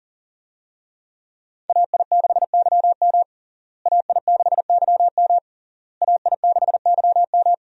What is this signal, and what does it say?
Morse code, heard as a keyed tone, about 40 wpm, 700 Hz. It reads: AI6YM AI6YM AI6YM
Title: USA Call Signs - 5 Characters - Repeated 3x - 40wpm